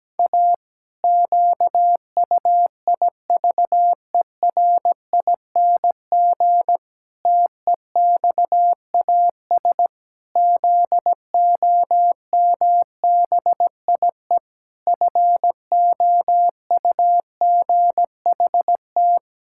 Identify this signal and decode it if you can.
Morse code, heard as a keyed tone, about 17 wpm, 700 Hz. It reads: A QUIVERING TEXAS ZOMBIE FOUGHT
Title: Pangram Challenge v2 - 17wpm